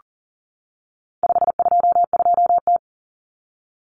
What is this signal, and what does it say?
Morse code, heard as a keyed tone, about 40 wpm, 700 Hz. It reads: 522T
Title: Common Cut Numbers - 4-Digit 40wpm